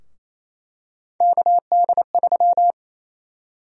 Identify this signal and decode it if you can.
Morse code, heard as a keyed tone, about 28 wpm, 700 Hz. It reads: KD3
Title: USA Call Sign Prefixes - 28wpm